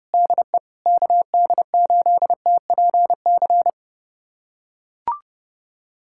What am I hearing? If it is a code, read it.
Morse code, heard as a keyed tone, about 30 wpm, 700 Hz. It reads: DE KD8TPC
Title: USA Callsign DE Callsign - 6 Characters - 30wpm